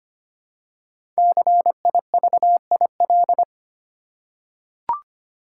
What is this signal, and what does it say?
Morse code, heard as a keyed tone, about 25 wpm, 700 Hz. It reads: CIVIL
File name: 5 Letter Words - First 3 Letters Given - 25wpm